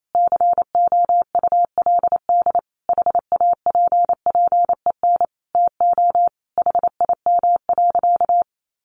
Morse code, heard as a keyed tone, about 28 wpm, 700 Hz. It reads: COULD HAPPEN TO HIM.
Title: Sentences from Top 400 Words 28wpm